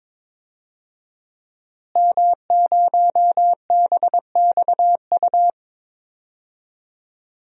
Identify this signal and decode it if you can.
Morse code, heard as a keyed tone, about 21 wpm, 700 Hz. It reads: M0BXU
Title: DXCC-100 Call Signs - 5 Characters - Repeated 3x - 22wpm